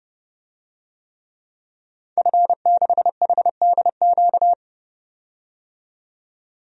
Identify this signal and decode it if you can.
Morse code, heard as a keyed tone, about 30 wpm, 700 Hz. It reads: F6HDQ
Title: DXCC-100 Call Signs - 5 Characters - 30wpm